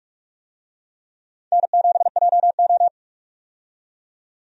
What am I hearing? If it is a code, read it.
Morse code, heard as a keyed tone, about 46 wpm, 700 Hz. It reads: N7JO
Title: USA Call Signs - 4 Characters - 45wpm